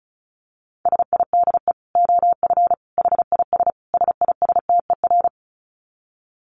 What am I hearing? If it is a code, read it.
Morse code, heard as a keyed tone, about 34 wpm, 700 Hz. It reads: SIDE OF HIS SISTER
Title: Dit Challenge 35wpm